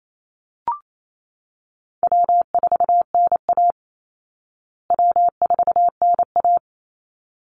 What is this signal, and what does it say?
Morse code, heard as a keyed tone, about 28 wpm, 700 Hz. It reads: W4NA W4NA
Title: Easy Call Signs Repeated 2x - 28wpm